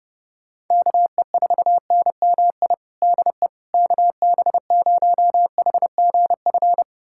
Morse code, heard as a keyed tone, about 30 wpm, 700 Hz. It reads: KE4NMI DE KB0HGF
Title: USA Callsign DE Callsign - 6 Characters - 30wpm